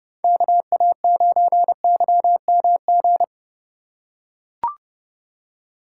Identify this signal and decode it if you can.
Morse code, heard as a keyed tone, about 32 wpm, 700 Hz. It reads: KA9YMG
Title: K1USN SST QSO Practice - 30wpm